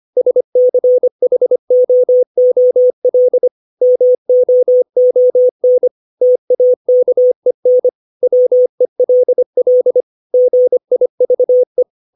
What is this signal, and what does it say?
Morse code, heard as a keyed tone, about 25 wpm, 500 Hz. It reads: SCHOOL MOON TAKEN WELL GIVE
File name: 632 words / 632 Words 25WPM